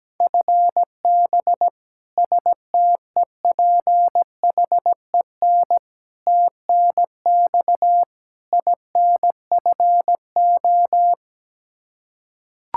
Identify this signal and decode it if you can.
Morse code, heard as a keyed tone, about 17 wpm, 700 Hz. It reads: FB STEPHEN TNX INFO
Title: Practice QSO - Exchange 3 - More Info - 17wpm